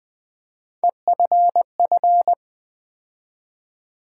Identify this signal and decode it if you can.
Morse code, heard as a keyed tone, about 20 wpm, 700 Hz. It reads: EFF
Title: Build A Word - 3 Letters At A Time - 9-Letter Words - 20wpm